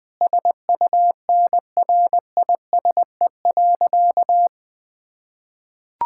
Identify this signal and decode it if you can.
Morse code, heard as a keyed tone, about 20 wpm, 700 Hz. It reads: SUNRISE.
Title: Difficult Sentences with Hints - 20wpm